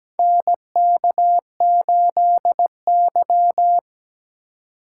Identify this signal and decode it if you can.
Morse code, heard as a keyed tone, about 17 wpm, 700 Hz. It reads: NK8Y
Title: USA Call Signs - 4 Characters - 3x Character Spacing - 17wpm